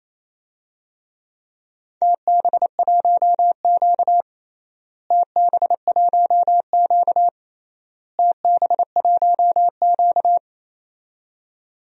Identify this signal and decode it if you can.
Morse code, heard as a keyed tone, about 28 wpm, 700 Hz. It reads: TB1Q TB1Q TB1Q
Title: DXCC-100 Call Signs - 4 Characters - Repeated 3x - 28wpm